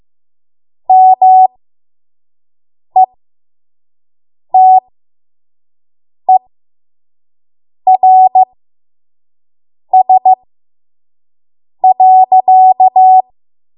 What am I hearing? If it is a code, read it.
Morse code, heard as a keyed tone, about 15 wpm, spaced as for 5 wpm, 750 Hz. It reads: METERS.
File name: Morse / 05WPM